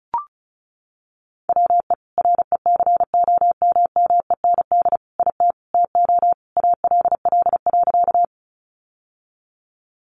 Morse code, heard as a keyed tone, about 34 wpm, 700 Hz. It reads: WE RECOMMEND IT TO ALL.
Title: Sentences from Top 1200 Words 35wpm